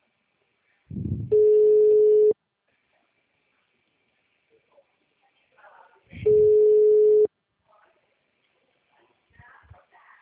Ringback tone (Europe, 425 Hz).